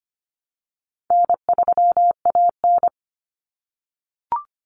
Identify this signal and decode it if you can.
Morse code, heard as a keyed tone, about 25 wpm, 700 Hz. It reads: N3AN
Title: Common Cut Numbers - 4-Digit 25wpm